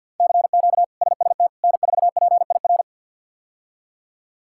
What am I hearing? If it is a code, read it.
Morse code, heard as a keyed tone, about 51 wpm, 700 Hz. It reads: CQ SST N4PIR
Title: K1USN SST QSO Practice - 50wpm